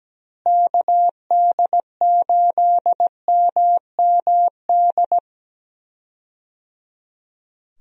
Morse code, heard as a keyed tone, about 17 wpm, 700 Hz. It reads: KD8MMD
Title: USA Call Signs - 6 Characters - Repeated 3x - 17wpm